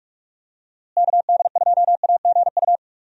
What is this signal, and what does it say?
Morse code, heard as a keyed tone, about 46 wpm, 700 Hz. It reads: KD2AGU
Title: USA Call Signs - 6 Characters - Repeated 2x - 45wpm